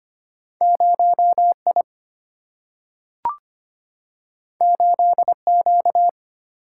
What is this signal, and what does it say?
Morse code, heard as a keyed tone, about 25 wpm, 700 Hz. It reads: 0I 8Q
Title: Single-Letter Single-Number 25wpm